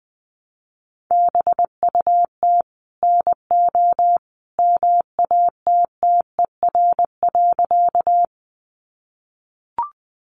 Morse code, heard as a keyed tone, about 20 wpm, 700 Hz. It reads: BUT NO MATTER.
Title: Sentences from Top 700 Words 20wpm